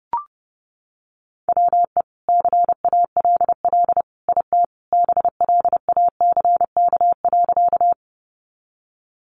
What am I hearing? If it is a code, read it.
Morse code, heard as a keyed tone, about 30 wpm, 700 Hz. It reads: WE CALL IT BLACK.
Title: Sentences from Top 400 Words 30wpm